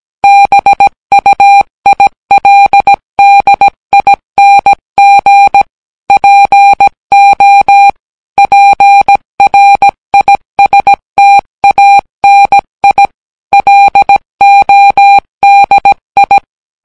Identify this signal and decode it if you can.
Morse code, heard as a keyed tone, about 17 wpm, 800 Hz. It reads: BUILDING PO PRISTANI LODI